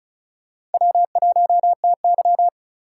Morse code, heard as a keyed tone, about 34 wpm, 700 Hz. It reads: W1TY
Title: USA Call Signs - 3x Character Spacing - 35wpm